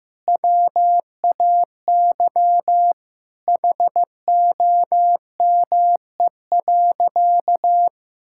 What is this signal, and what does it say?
Morse code, heard as a keyed tone, about 15 wpm, 700 Hz. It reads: WAY HOME.
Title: Sentences from Top 1600 Words 15wpm